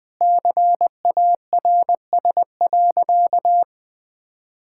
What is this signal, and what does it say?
Morse code, heard as a keyed tone, about 20 wpm, 700 Hz. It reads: CARS.
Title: Sentences with 5-Word Phrases - 20wpm